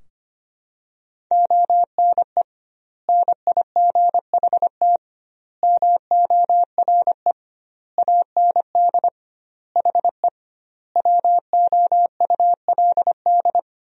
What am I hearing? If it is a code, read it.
Morse code, heard as a keyed tone, about 25 wpm, 700 Hz. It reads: ONE NIGHT MORE AND HE WOULD
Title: 2x Word-Spacing - Sentences from Top 300 Words - 25wpm